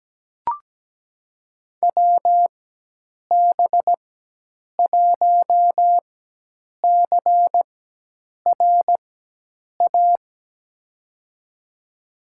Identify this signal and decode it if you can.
Morse code, heard as a keyed tone, about 17 wpm, spaced as for 8 wpm, 700 Hz. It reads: WB1CRA
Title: USA Call Signs - 4x Character Spacing - 17wpm